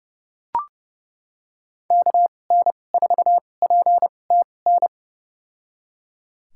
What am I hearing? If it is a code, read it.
Morse code, heard as a keyed tone, about 30 wpm, spaced as for 22 wpm, 700 Hz. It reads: KN4PTN
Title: USA Call Signs - 6 Characters - 2x Character Spacing - 30wpm